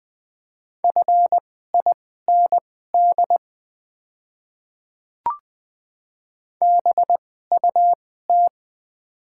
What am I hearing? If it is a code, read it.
Morse code, heard as a keyed tone, about 20 wpm, spaced as for 14 wpm, 690 Hz. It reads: FIND BUT
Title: 2x Character Spacing - Top 100 Words - 20wpm